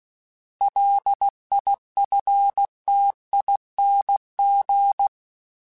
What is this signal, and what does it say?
Morse code, heard as a keyed tone, about 16 wpm, 800 Hz. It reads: LIFTING
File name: Alphabetical / Alphabetical mp3s / Ls @15